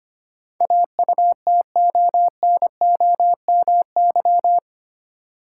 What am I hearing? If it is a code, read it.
Morse code, heard as a keyed tone, about 25 wpm, 700 Hz. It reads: AUTONOMY
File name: 8 Letter Words - First 3 Letters Given - 25wpm